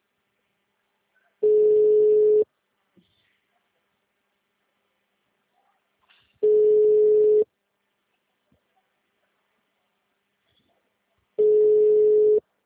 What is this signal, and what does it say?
Ringback tone (Europe, 425 Hz).